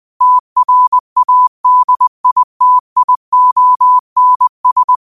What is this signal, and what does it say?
Morse code, heard as a keyed tone, about 20 wpm, 1000 Hz. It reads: TRADITIONS
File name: qotdfiles / 20wpm